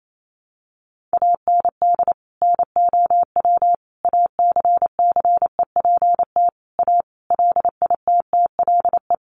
Morse code, heard as a keyed tone, about 28 wpm, 700 Hz. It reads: AND NOW ACCEPT A LITTLE
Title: Sentences from Top 1300 Words 28wpm